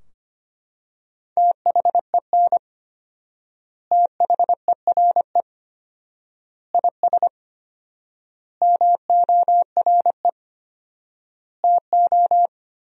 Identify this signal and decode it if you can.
Morse code, heard as a keyed tone, about 25 wpm, 700 Hz. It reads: THEN THERE IS MORE TO
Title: 4x Word-Spacing - Sentences from Top 100 Words - 25wpm